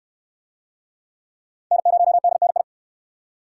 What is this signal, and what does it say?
Morse code, heard as a keyed tone, about 69 wpm, 700 Hz. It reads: N0DNI